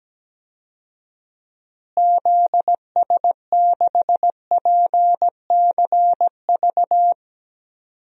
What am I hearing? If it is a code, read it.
Morse code, heard as a keyed tone, about 17 wpm, 700 Hz. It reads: ZS6PCV
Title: DXCC-100 Call Signs - 6 Characters - Repeated 2x - 17wpm